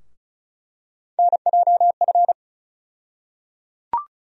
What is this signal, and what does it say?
Morse code, heard as a keyed tone, about 34 wpm, 700 Hz. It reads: NJF
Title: USA Call Sign Postfixes - 3 Letters 35wpm